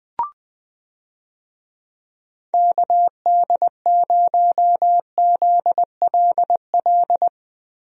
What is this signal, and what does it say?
Morse code, heard as a keyed tone, about 20 wpm, 700 Hz. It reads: KD0ZLL
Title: USA Call Signs - 6 Characters - Repeated 2x - 20wpm